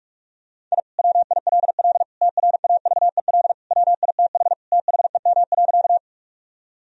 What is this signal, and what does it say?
Morse code, heard as a keyed tone, about 46 wpm, 700 Hz. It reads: I WILL TRAVEL WITH THEM.
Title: Sentences from Top 1000 Words - Review Part 2 - 45wpm